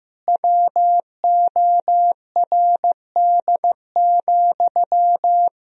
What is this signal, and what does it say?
Morse code, heard as a keyed tone, about 16 wpm, 700 Hz. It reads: WORD,
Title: Sentences from Top 1000 Words - Review Part 1 - 15wpm